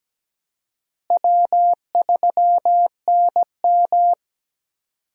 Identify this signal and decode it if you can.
Morse code, heard as a keyed tone, about 17 wpm, 700 Hz. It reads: W3NM
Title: USA Call Signs - 4 Characters - Repeated 3x - 17wpm